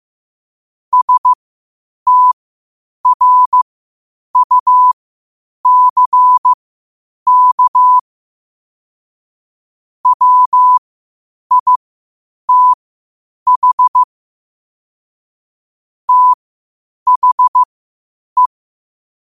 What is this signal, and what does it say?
Morse code, heard as a keyed tone, about 15 wpm, spaced as for 8 wpm, 1000 Hz. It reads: STRUCK WITH THE
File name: qotdfiles / 8wpm